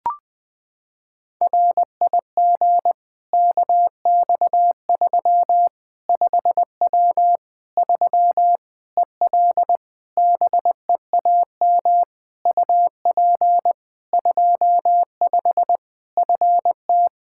Morse code, heard as a keyed tone, about 21 wpm, 700 Hz. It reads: RIG KX3 5W 3 EL BEAM UP 25 FT
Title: K1USN SST QSO Practice 20wpm